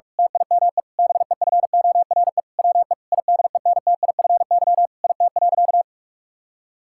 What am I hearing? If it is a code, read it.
Morse code, heard as a keyed tone, about 46 wpm, 700 Hz. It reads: TIME BEFORE WE IDENTIFY IT.
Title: Sentences from Top 1000 Words - Review Part 6 - 45wpm